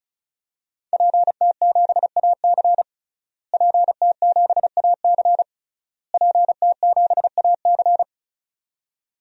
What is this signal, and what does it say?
Morse code, heard as a keyed tone, about 36 wpm, 700 Hz. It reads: PT7AC PT7AC PT7AC
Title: DXCC-100 Call Signs - 5 Characters - Repeated 3x - 35wpm